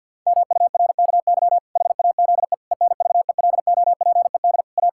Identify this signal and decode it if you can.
Morse code, heard as a keyed tone, about 50 wpm, 700 Hz. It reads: MURKY HAZE ENVELOPED A
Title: Pangram Challenge v2 - 50wpm